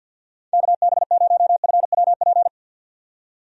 Morse code, heard as a keyed tone, about 47 wpm, 700 Hz. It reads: KB0FWP
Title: USA Call Signs - 6 Characters - 50wpm